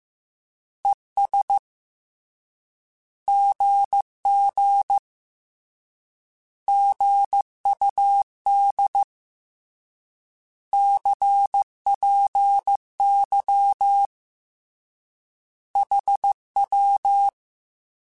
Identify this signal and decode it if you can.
Morse code, heard as a keyed tone, about 15 wpm, 800 Hz. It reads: ES GG GUD CPY HW
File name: Short Runs / Short Runs-mp3s / Words @13